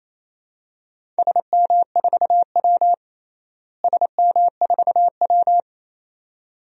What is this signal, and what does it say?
Morse code, heard as a keyed tone, about 28 wpm, 700 Hz. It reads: SM4W SM4W
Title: DXCC-100 Call Signs - 4 Characters - Repeated 2x - 28wpm